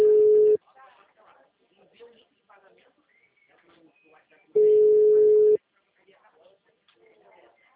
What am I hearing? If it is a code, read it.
Ringback tone (Europe, 425 Hz), with speech.